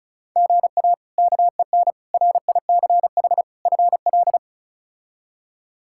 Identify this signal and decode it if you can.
Morse code, heard as a keyed tone, about 36 wpm, 700 Hz. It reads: GA KEN RICH FL
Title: K1USN SST QSO Practice - 35wpm